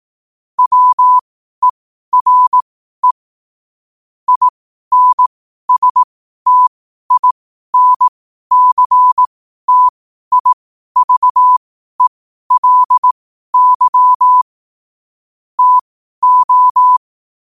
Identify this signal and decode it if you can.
Morse code, heard as a keyed tone, about 18 wpm, spaced as for 12 wpm, 1000 Hz. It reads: WERE INSTINCTIVELY TO
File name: qotdfiles / 12wpm